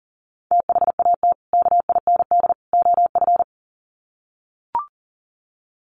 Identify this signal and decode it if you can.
Morse code, heard as a keyed tone, about 40 wpm, 700 Hz. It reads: THAT KIND OF
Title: Sets of 3 Words 40wpm